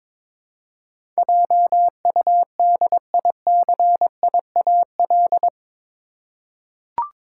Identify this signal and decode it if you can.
Morse code, heard as a keyed tone, about 22 wpm, 700 Hz. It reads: JUDICIAL
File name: practice / 22wpm / Eight Letter Words - 22wpm